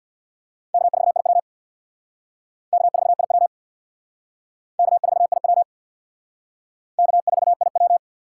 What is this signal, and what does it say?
Morse code, heard as a keyed tone, about 60 wpm, 700 Hz. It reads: K4IW K4IW K4IW K4IW
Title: Speed Racing - USA 1x2 Call Signs 50wpm